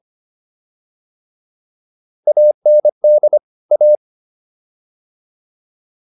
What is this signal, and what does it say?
Morse code, heard as a keyed tone, about 25 wpm, 600 Hz. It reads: AND A